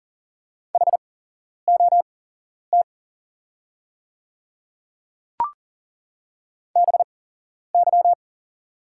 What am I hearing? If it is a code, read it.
Morse code, heard as a keyed tone, about 40 wpm, spaced as for 11 wpm, 690 Hz. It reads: HOT BY
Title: 8x Character Spacing - Top 200 Words - 40wpm